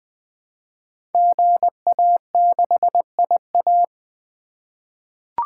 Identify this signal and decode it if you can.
Morse code, heard as a keyed tone, about 20 wpm, 700 Hz. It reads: GA6IA
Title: Code Groups - 5 Letter:Numbers - 20wpm